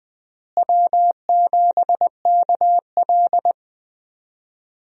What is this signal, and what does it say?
Morse code, heard as a keyed tone, about 20 wpm, 700 Hz. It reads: W7KL
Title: USA Call Signs - 4 Characters - Repeated 2x - 20wpm